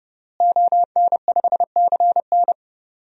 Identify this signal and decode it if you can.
Morse code, heard as a keyed tone, about 32 wpm, 700 Hz. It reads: ON5CN
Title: DXCC-100 Call Signs - 5 Characters - Repeated 3x - 30wpm